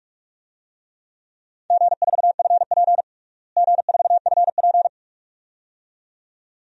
Morse code, heard as a keyed tone, about 46 wpm, 700 Hz. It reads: G4FP G4FP
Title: DXCC-100 Call Signs - 4 Characters - Repeated 2x - 45wpm